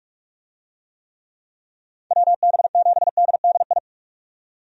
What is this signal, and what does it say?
Morse code, heard as a keyed tone, about 46 wpm, 700 Hz. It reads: WB7DDI
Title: USA Call Signs - 6 Characters - 45wpm